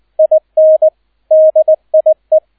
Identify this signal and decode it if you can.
Morse code, heard as a keyed tone, about 20 wpm, 630 Hz, with radio noise. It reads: IN DIE